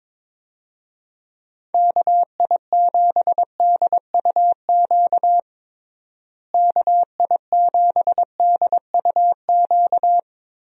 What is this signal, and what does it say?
Morse code, heard as a keyed tone, about 22 wpm, 700 Hz. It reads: KI7DUQ KI7DUQ
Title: USA Call Signs - 6 Characters - Repeated 3x - 22wpm